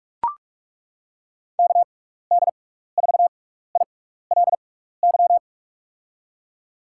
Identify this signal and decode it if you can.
Morse code, heard as a keyed tone, about 42 wpm, spaced as for 15 wpm, 700 Hz. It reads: KD4ILY
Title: USA Call Signs - 6 Characters - 6x Character Spacing - 45wpm